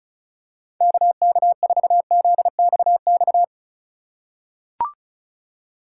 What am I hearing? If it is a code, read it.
Morse code, heard as a keyed tone, about 36 wpm, 700 Hz. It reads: KK4ZXX
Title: USA Call Signs - 6 Characters - 3x Character Spacing - 35wpm